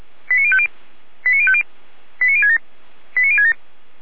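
Five-tone selective call (ZVEI), heard as quick runs of five tones, tones of about 70 ms. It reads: 89050 89050 89067 89067
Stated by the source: ZVEI five-tone call received by SDR.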